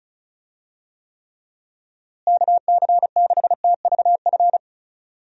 Morse code, heard as a keyed tone, about 34 wpm, 700 Hz. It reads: KC6TVF
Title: USA Call Signs - 6 Characters - Repeated 2x - 35wpm